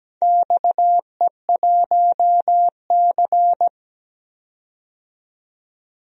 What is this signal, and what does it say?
Morse code, heard as a keyed tone, about 17 wpm, 700 Hz. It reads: XE1C
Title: DXCC-100 Call Signs - 4 Characters - 17wpm